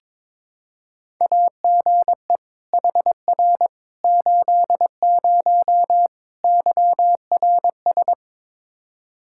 Morse code, heard as a keyed tone, about 22 wpm, 700 Hz. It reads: AGE HR 80 YRS
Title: K1USN SST QSO Practice 22wpm